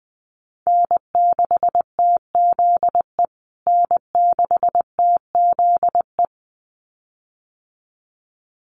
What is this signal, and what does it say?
Morse code, heard as a keyed tone, about 20 wpm, 700 Hz. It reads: N6TZE N6TZE
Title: USA Call Signs Repeated 2x 20wpm